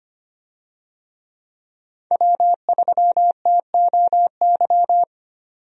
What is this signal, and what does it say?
Morse code, heard as a keyed tone, about 25 wpm, 700 Hz. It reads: W3TOY
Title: USA Call Signs - 5 Characters - 25wpm